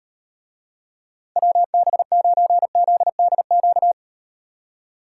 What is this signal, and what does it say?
Morse code, heard as a keyed tone, about 38 wpm, 700 Hz. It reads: WB9ZDQ